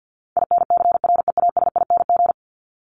Morse code, heard as a keyed tone, about 50 wpm, 700 Hz. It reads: INCREASING
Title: ING Suffix - Top 100 Words with Base Words 50wpm